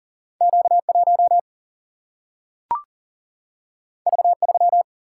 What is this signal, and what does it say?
Morse code, heard as a keyed tone, about 41 wpm, 700 Hz. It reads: Q1 V3
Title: Single-Letter Single-Number 40wpm